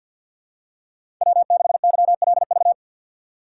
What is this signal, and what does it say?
Morse code, heard as a keyed tone, about 48 wpm, 700 Hz. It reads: W6YLV
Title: USA Call Signs - 2x Character Spacing - 50wpm